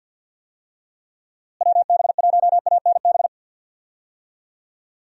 Morse code, heard as a keyed tone, about 52 wpm, 700 Hz. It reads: WB1ANB
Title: USA Call Signs - 6 Characters - Repeated 3x - 50wpm